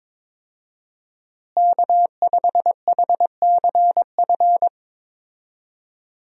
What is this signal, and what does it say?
Morse code, heard as a keyed tone, about 22 wpm, 690 Hz. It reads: K5HCF
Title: USA Call Signs - 5 Characters - Repeated 3x - 22wpm